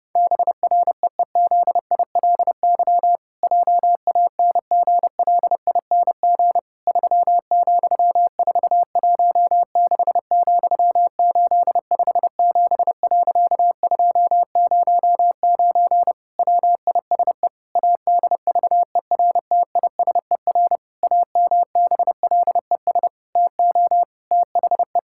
Morse code, heard as a keyed tone, about 30 wpm, 700 Hz. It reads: BREEZILY JANGLING 3,416,857.209 WISE ADVERTISER AMBLES TO THE
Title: Pangram Challenge v2 - 30wpm